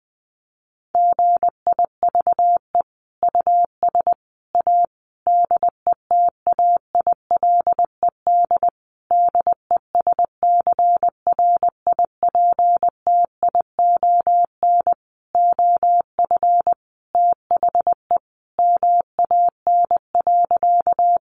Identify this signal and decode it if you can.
Morse code, heard as a keyed tone, about 19 wpm, 700 Hz. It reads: GIVE US A DETAILED DESCRIPTION OF THE MAN.
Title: Sentences from Top 1500 Words 20wpm